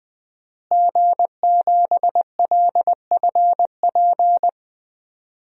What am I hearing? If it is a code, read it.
Morse code, heard as a keyed tone, about 20 wpm, 700 Hz. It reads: G7LFP
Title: DXCC-100 Call Signs - 5 Characters - Repeated 3x - 20wpm